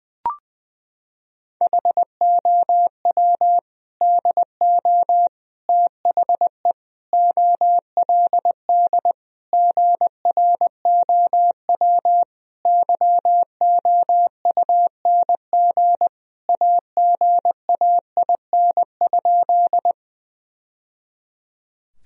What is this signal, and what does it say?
Morse code, heard as a keyed tone, about 21 wpm, 700 Hz. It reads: HOW DO THE OLD GROW YOUNG AGAIN?
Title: Sentences from Top 1000 Words - Review Part 1 - 20wpm